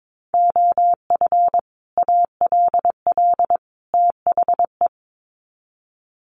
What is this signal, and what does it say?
Morse code, heard as a keyed tone, about 22 wpm, 700 Hz. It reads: OF ALL THE
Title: Sets of 3 Words 22wpm